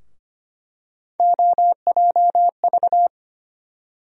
Morse code, heard as a keyed tone, about 25 wpm, 700 Hz. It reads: OJV
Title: USA Call Sign Postfixes - 3 Letters 25wpm